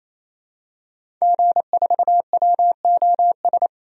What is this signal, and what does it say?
Morse code, heard as a keyed tone, about 28 wpm, 700 Hz. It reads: G4WOS
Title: DXCC-100 Call Signs - 5 Characters - 28wpm